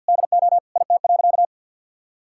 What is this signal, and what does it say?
Morse code, heard as a keyed tone, about 50 wpm, 700 Hz. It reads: DO IT.
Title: Sentences from Top 1000 Words - Review Part 6 - 50wpm